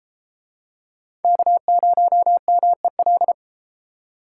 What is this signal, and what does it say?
Morse code, heard as a keyed tone, about 30 wpm, 700 Hz. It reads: K0MEL